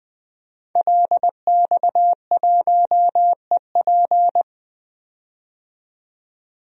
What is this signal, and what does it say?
Morse code, heard as a keyed tone, about 20 wpm, 700 Hz. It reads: LX1EP
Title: DXCC-100 Call Signs - 5 Characters - 20wpm